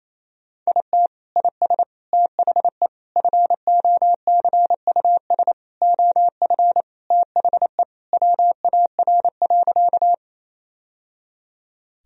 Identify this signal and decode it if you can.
Morse code, heard as a keyed tone, about 28 wpm, 700 Hz. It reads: IT IS THE FOCUS OF THE WAR.
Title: Sentences from Top 1000 Words - Review Part 4 - 28wpm